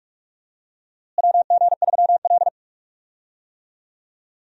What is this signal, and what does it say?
Morse code, heard as a keyed tone, about 42 wpm, 700 Hz. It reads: WG3L
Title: USA Call Signs - 4 Characters - Repeated 3x - 45wpm